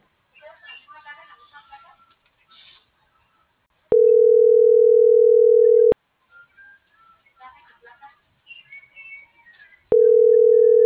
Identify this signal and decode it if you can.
Ringback tone (North America), 440 + 480 Hz.